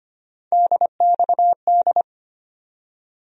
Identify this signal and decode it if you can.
Morse code, heard as a keyed tone, about 25 wpm, 700 Hz. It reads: DXD
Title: USA Call Sign Postfixes - 3 Letters 25wpm